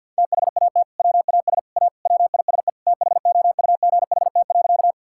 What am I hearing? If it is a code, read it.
Morse code, heard as a keyed tone, about 50 wpm, 700 Hz. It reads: THAT WAS A WISE THOUGHT.
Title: Sentences from Top 1600 Words 50wpm